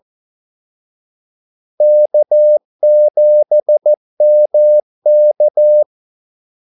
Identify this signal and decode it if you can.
Morse code, heard as a keyed tone, about 14 wpm, 600 Hz. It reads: K7MK
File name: Callsigns-14wpm